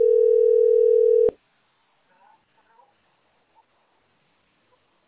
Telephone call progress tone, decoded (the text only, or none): ringback tone